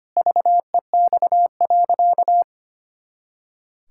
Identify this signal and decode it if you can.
Morse code, heard as a keyed tone, about 25 wpm, 700 Hz. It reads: VEX.
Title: Pangram Challenge v2 - 25wpm